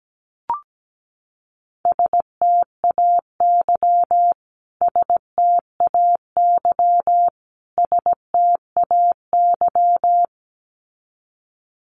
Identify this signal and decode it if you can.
Morse code, heard as a keyed tone, about 17 wpm, 700 Hz. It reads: STAY STAY STAY
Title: Top 500 Words Repeated 17wpm